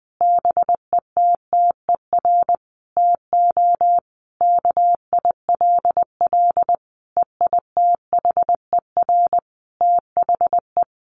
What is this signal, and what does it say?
Morse code, heard as a keyed tone, about 19 wpm, 700 Hz. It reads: BETTER TO KILL EITHER THE
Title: Sentences from Top 1500 Words 20wpm